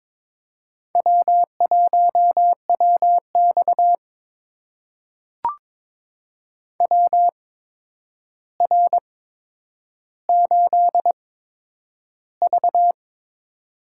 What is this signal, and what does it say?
Morse code, heard as a keyed tone, about 22 wpm, spaced as for 9 wpm, 700 Hz. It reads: W1WX WR8V
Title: USA Call Signs - 4 Characters - 8x Character Spacing - 22wpm